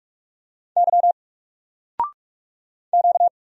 Morse code, heard as a keyed tone, about 45 wpm, 700 Hz. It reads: Y Q
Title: Single Letter-Number Alt 45wpm